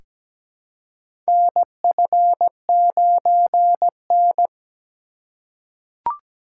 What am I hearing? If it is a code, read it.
Morse code, heard as a keyed tone, about 17 wpm, 700 Hz. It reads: NF9N
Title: USA Call Signs - 4 Characters - 4x Character Spacing - 17wpm